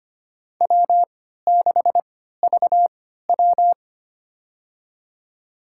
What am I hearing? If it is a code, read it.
Morse code, heard as a keyed tone, about 25 wpm, spaced as for 14 wpm, 700 Hz. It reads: W6VW
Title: USA Call Signs - 3x Character Spacing - 25wpm